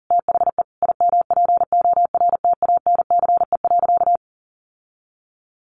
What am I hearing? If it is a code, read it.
Morse code, heard as a keyed tone, about 40 wpm, 700 Hz. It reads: THE IMPORTANCE.
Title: Sentences from Top 1200 Words 40wpm